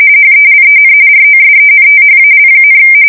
RTTY, about 45.45 baud. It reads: HELEN ROSENTHAL,
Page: 170 Shift 2125/2295